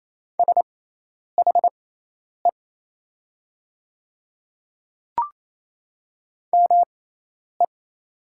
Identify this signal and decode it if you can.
Morse code, heard as a keyed tone, about 28 wpm, spaced as for 10 wpm, 700 Hz. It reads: SHE ME
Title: 6x Character Spacing - Top 200 Words - 28wpm